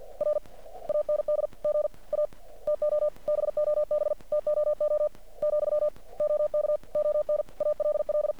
Morse code, heard as a keyed tone, about 50 wpm, 600 Hz. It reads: R AND GA TO BOB TOO , OK ON ALL
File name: archived / CRSnet